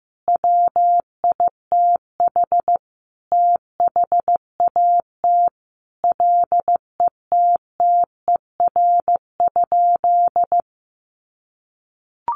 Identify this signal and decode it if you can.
Morse code, heard as a keyed tone, about 15 wpm, 700 Hz. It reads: WITH THAT LETTER?
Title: Sentences from Top 300 Words 15wpm